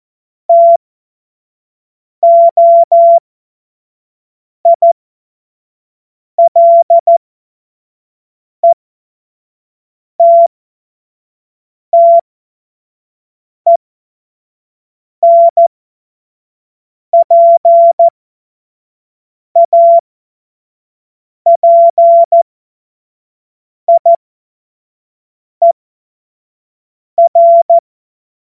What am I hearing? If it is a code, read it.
Morse code, heard as a keyed tone, about 14 wpm, spaced as for 5 wpm, 680 Hz. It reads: TOILETTENPAPIER
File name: Morsekurs / Witze / Witz 01. Juni 2021, 25 F70